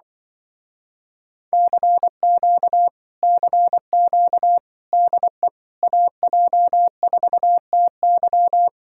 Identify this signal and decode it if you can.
Morse code, heard as a keyed tone, about 25 wpm, 700 Hz. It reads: CQ CQ DE AJ4TY
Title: Speed-Racing Common QSO Phrases - 20wpm